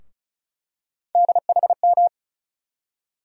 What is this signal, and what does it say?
Morse code, heard as a keyed tone, about 36 wpm, 700 Hz. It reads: DHM
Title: USA Call Sign Postfixes - 3 Letters 35wpm